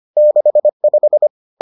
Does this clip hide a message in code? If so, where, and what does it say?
Yes: Morse code from 0.2 to 1.3 s, heard as a keyed tone, about 25 wpm, 600 Hz. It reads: 65